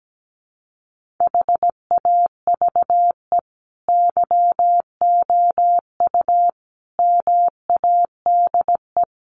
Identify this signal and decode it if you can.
Morse code, heard as a keyed tone, about 17 wpm, 700 Hz. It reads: HAVE YOU MADE
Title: Sentences from Top 1300 Words 17wpm